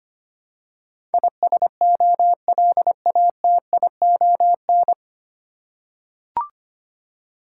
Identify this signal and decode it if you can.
Morse code, heard as a keyed tone, about 25 wpm, 700 Hz. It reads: ISOLATION
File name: Nine Letter Words - 25wpm